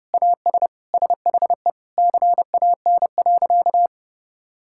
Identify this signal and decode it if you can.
Morse code, heard as a keyed tone, about 29 wpm, 700 Hz. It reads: AS SHE CAN.
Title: Sentences from Top 1000 Words - Review Part 5 - 30wpm